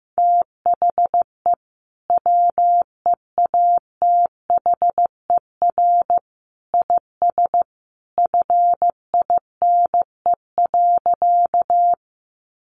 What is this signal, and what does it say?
Morse code, heard as a keyed tone, about 15 wpm, 700 Hz. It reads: THE WEATHER IS FINE.
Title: Sentences from Top 800 Words 15wpm